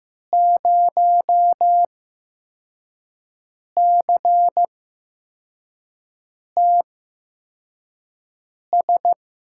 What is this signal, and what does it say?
Morse code, heard as a keyed tone, about 15 wpm, spaced as for 4 wpm, 700 Hz. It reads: 0CTS
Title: USA Call Signs - 6 Characters - 8x Character Spacing - 15wpm